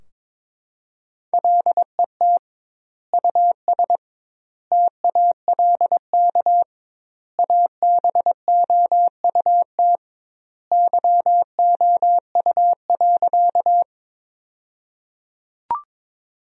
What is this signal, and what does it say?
Morse code, heard as a keyed tone, about 22 wpm, 700 Hz. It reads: LET US TALK ABOUT YOU.
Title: 2x Word-Spacing - Sentences from Top 400 Words - 22wpm